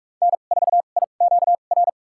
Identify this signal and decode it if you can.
Morse code, heard as a keyed tone, about 41 wpm, spaced as for 31 wpm, 700 Hz. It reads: N4IQR
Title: USA Call Signs - 5 Characters - 2x Character Spacing - 45wpm